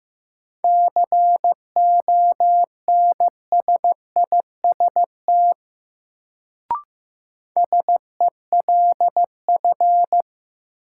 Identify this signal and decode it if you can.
Morse code, heard as a keyed tone, about 15 wpm, 700 Hz. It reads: CONSIST SELF
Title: Sentences from Top 1400 Words - New Word Drill Alt - 15wpm